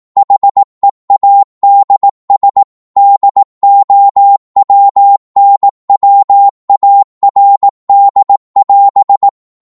Morse code, heard as a keyed tone, about 18 wpm, 800 Hz. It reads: HEADS DOWNWARD<AS>